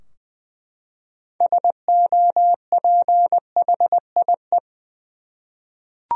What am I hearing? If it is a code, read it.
Morse code, heard as a keyed tone, about 20 wpm, 700 Hz. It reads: SOPHIE
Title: USA Common Names - 20wpm